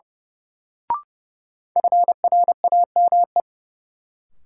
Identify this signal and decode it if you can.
Morse code, heard as a keyed tone, about 30 wpm, 700 Hz. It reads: FRAME
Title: Sentences from Top 1000 Words - Review Part 6 - New Word Drill - 30wpm